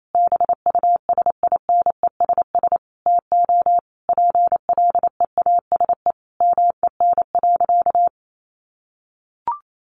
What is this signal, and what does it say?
Morse code, heard as a keyed tone, about 28 wpm, 700 Hz. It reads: BUSINESS TO PLEASE MEN.
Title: Sentences from Top 1500 Words 28wpm